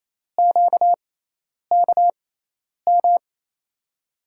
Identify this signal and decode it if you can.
Morse code, heard as a keyed tone, about 28 wpm, spaced as for 10 wpm, 700 Hz. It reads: QKM
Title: USA Call Signs - 6x Character Spacing - 28wpm